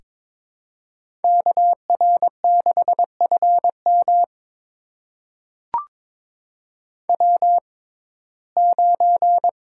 Morse code, heard as a keyed tone, about 22 wpm, spaced as for 16 wpm, 700 Hz. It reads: KR6FM W9
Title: USA Call Signs - 5 Characters - 6x Character Spacing - 22wpm